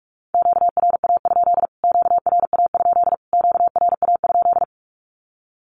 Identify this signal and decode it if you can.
Morse code, heard as a keyed tone, about 44 wpm, 700 Hz. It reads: QRA? QRA? QRA?
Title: Q-Codes Repeated 3x 45wpm